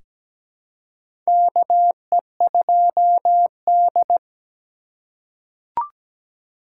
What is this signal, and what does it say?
Morse code, heard as a keyed tone, about 17 wpm, 700 Hz. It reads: KE2D
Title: K1USN SST QSO Practice - 17wpm